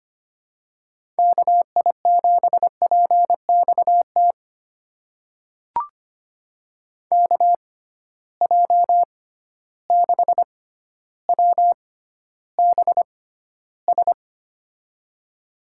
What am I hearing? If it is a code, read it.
Morse code, heard as a keyed tone, about 25 wpm, spaced as for 13 wpm, 700 Hz. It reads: KI7PXT KJ6WBS
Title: USA Call Signs - 6 Characters - 6x Character Spacing - 25wpm